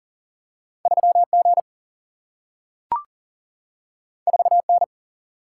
Morse code, heard as a keyed tone, about 39 wpm, 700 Hz. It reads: 3 G 4 N
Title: Single-Letter Single-Number 40wpm